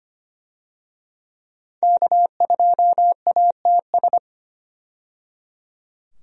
Morse code, heard as a keyed tone, about 25 wpm, 690 Hz. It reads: K2ATS
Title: USA Call Signs - 5 Characters - 25wpm